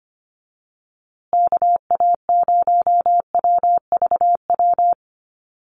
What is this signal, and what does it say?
Morse code, heard as a keyed tone, about 25 wpm, 700 Hz. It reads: KA0WVW
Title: USA Call Signs Repeated 3x 25wpm